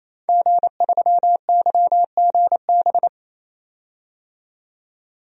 Morse code, heard as a keyed tone, about 28 wpm, 700 Hz. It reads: G3YGB
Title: DXCC-100 Call Signs - 5 Characters - 28wpm